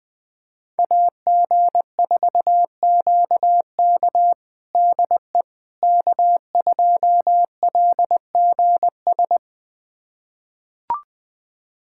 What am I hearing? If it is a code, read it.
Morse code, heard as a keyed tone, about 20 wpm, 700 Hz. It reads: AG4QK DE K2LGS
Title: USA Callsign DE Callsign - 5 Characters - 20wpm